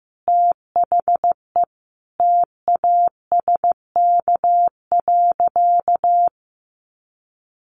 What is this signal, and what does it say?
Morse code, heard as a keyed tone, about 15 wpm, 700 Hz. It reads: THE TASK.
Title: Sentences from Top 1400 Words 15wpm